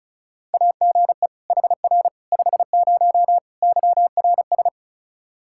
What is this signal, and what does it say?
Morse code, heard as a keyed tone, about 36 wpm, 700 Hz. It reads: AGE HR 50 YRS
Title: K1USN SST QSO Practice 35wpm